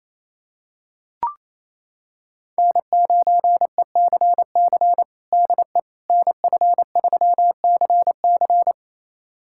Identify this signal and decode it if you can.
Morse code, heard as a keyed tone, about 28 wpm, 700 Hz. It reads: N9ECC DE NF3CC
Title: USA Callsign DE Callsign 28wpm